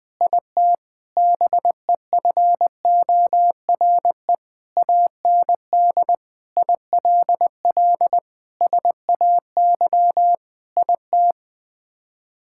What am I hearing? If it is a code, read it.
Morse code, heard as a keyed tone, about 21 wpm, 700 Hz. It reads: IT BEFORE AND ILL SAY IT
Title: Sets of 8 Words - 20wpm